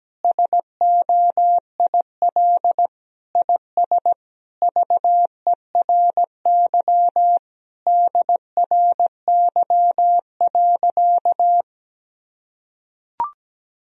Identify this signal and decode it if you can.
Morse code, heard as a keyed tone, about 17 wpm, 700 Hz. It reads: SOIL IS VERY DRY.
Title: Sentences from Top 1000 Words - Review Part 4 - 17wpm